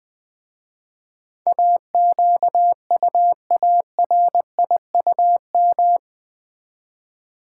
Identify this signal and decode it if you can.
Morse code, heard as a keyed tone, about 20 wpm, 700 Hz. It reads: AQUARIUM
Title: Easy Words with Hints - 20wpm